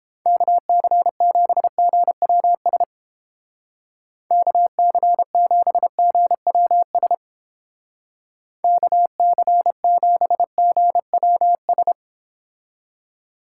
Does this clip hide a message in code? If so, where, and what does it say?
Yes: Morse code from 0.3 to 11.9 s, heard as a keyed tone, about 29 wpm, 700 Hz. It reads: KC7GWS KC7GWS KC7GWS